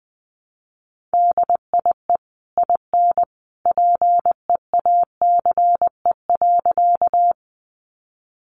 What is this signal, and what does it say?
Morse code, heard as a keyed tone, about 20 wpm, 700 Hz. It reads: DIE IN PEACE.
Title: Sentences from Top 1500 Words 20wpm